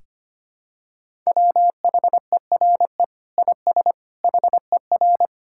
Morse code, heard as a keyed tone, about 25 wpm, 700 Hz. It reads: WHERE IS HER
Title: Sentences from Top 1000 Words - Review Part 5 - 25wpm